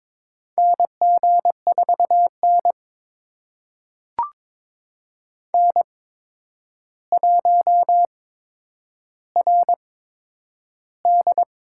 Morse code, heard as a keyed tone, about 21 wpm, spaced as for 9 wpm, 700 Hz. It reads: NG4N N1RD
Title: USA Call Signs - 4 Characters - 8x Character Spacing - 22wpm